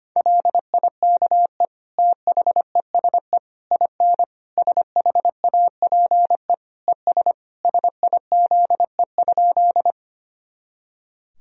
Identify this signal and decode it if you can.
Morse code, heard as a keyed tone, about 25 wpm, 700 Hz. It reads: LIKE THESE IN SHAPE ES SIZE?
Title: Sentences from Top 500 Words and Abbreviations 25wpm